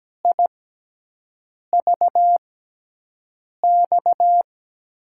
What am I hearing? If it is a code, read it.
Morse code, heard as a keyed tone, about 17 wpm, spaced as for 6 wpm, 700 Hz. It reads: IVX
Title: USA Call Signs - 6 Characters - 6x Character Spacing - 17wpm